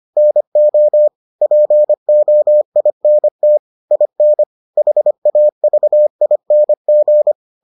Morse code, heard as a keyed tone, about 25 wpm, 600 Hz. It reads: NO POINT IN HAVING